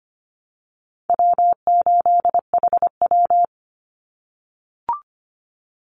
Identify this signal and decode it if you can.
Morse code, heard as a keyed tone, about 25 wpm, 700 Hz. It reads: W8HW
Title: Call Signs - 25wpm